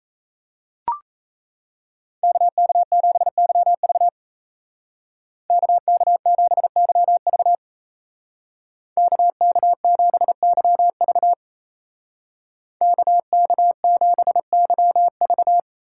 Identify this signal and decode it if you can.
Morse code, heard as a keyed tone, about 37 wpm, 700 Hz. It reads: KK7YV KK7YV KK7YV KK7YV